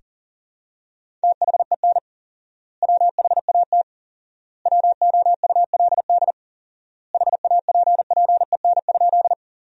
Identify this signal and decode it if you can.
Morse code, heard as a keyed tone, about 40 wpm, 700 Hz. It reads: THEN WHAT WOULD HAPPEN?
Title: 4x Word-Spacing - Sentences from Top 400 Words - 40wpm